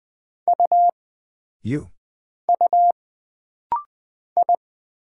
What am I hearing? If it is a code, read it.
Morse code, heard as a keyed tone, about 20 wpm, 700 Hz. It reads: U U I
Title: Single Character (Letter:Number:Punctuation:Prosign) - Rapid-Fire - 20wpm